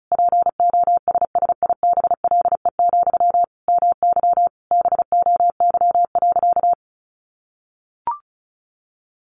Morse code, heard as a keyed tone, about 34 wpm, 700 Hz. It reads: POSSIBLE, MY BOY.
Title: Sentences from Top 800 Words 35wpm